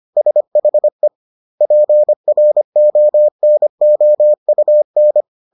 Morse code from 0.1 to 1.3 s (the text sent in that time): SHE